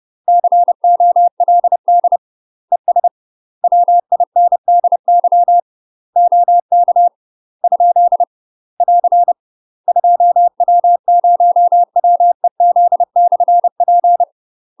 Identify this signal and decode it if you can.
Morse code, heard as a keyed tone, about 30 wpm, 700 Hz. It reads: COLD ES WINDY OK ? <AR> 2W0WEZ/P
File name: UK MASTER QSOs TRAINING / 30WPM / UK MASTER QSOs 0033